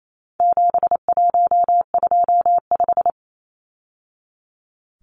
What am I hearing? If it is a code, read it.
Morse code, heard as a keyed tone, about 28 wpm, 700 Hz. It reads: 7125
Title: Numbers - 4-Digit 28wpm